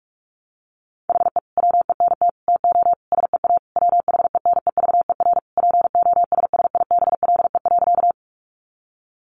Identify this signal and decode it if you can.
Morse code, heard as a keyed tone, about 44 wpm, 700 Hz. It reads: HE WENT TO SEA WHENEVER POSSIBLE.
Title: Sentences from Top 1200 Words 45wpm